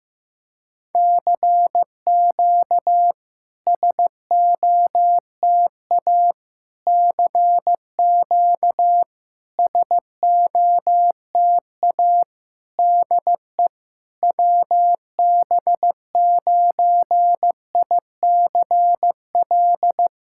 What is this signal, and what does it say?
Morse code, heard as a keyed tone, about 15 wpm, 700 Hz. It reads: CQ SOTA CQ SOTA DE WB9ICL
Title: SOTA QSO Practice 15wpm